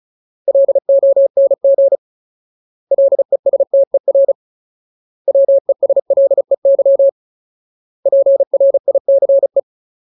Morse code, heard as a keyed tone, about 34 wpm, 550 Hz. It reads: LONG LESTER WESLEY PRICE
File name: Ultimate Word List / 35WPM Ultimate